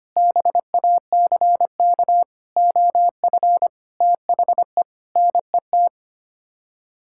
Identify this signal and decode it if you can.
Morse code, heard as a keyed tone, about 25 wpm, 700 Hz. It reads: BACK OF THE NET
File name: Sets of 9 Words - 25wpm